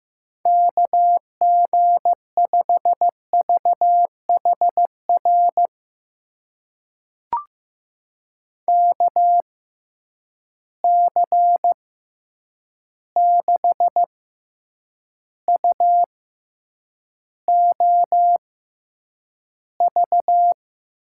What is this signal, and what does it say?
Morse code, heard as a keyed tone, about 15 wpm, spaced as for 8 wpm, 700 Hz. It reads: KG5VHR KC6UOV
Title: USA Call Signs - 6 Characters - 6x Character Spacing - 15wpm